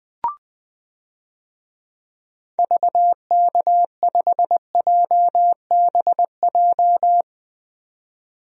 Morse code, heard as a keyed tone, about 20 wpm, 700 Hz. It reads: VK5JBJ
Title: DXCC-100 Call Signs - 6 Characters - 20wpm